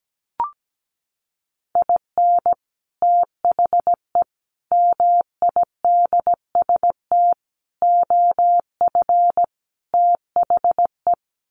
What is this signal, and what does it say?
Morse code, heard as a keyed tone, about 17 wpm, 700 Hz. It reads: IN THE MIDST OF THE
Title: Sets of 5 Words 17wpm